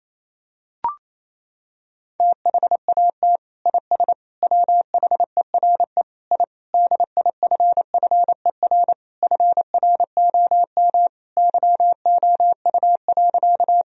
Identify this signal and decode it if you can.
Morse code, heard as a keyed tone, about 28 wpm, 700 Hz. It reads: THAT IS WHERE I DIFFER FROM YOU.
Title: Sentences from Top 1000 Words - Review Part 1 - 28wpm